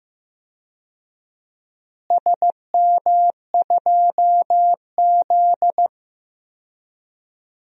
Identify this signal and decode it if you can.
Morse code, heard as a keyed tone, about 15 wpm, 700 Hz. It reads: SM2Z
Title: DXCC-100 Call Signs - 4 Characters - Repeated 3x - 15wpm